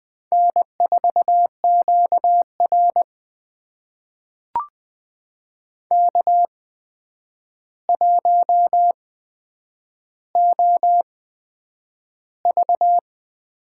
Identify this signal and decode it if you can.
Morse code, heard as a keyed tone, about 20 wpm, spaced as for 9 wpm, 700 Hz. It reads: N4QR K1OV
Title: USA Call Signs - 4 Characters - 8x Character Spacing - 20wpm